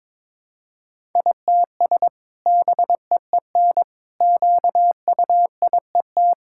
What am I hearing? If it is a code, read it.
Morse code, heard as a keyed tone, about 22 wpm, 700 Hz. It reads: ITS BEEN QUIET
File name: practice / 22wpm / Sentences with 2-Word Phrases - 22wpm